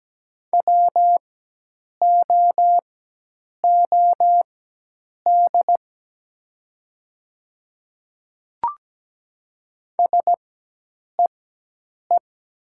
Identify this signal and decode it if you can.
Morse code, heard as a keyed tone, about 17 wpm, spaced as for 8 wpm, 700 Hz. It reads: WOOD SEE
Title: 4x Character Spacing - Top 300 Words - 17wpm